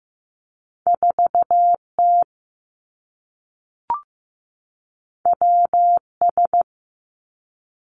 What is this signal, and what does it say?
Morse code, heard as a keyed tone, about 15 wpm, 700 Hz. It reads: 4 T W S
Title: Two Challenging Letter-Number Combinations 15wpm